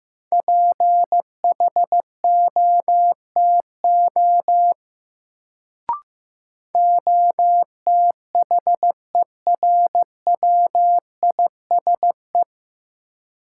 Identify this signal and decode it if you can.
Morse code, heard as a keyed tone, about 15 wpm, 700 Hz. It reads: PHOTO OTHERWISE
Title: Sentences from Top 1200 Words - New Word Drill Alt - 15wpm